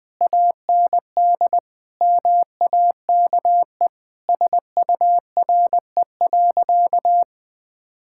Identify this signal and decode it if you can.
Morse code, heard as a keyed tone, about 20 wpm, 700 Hz. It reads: AND MAKE SURE.
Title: Sentences from Top 1000 Words - Review Part 5 - 20wpm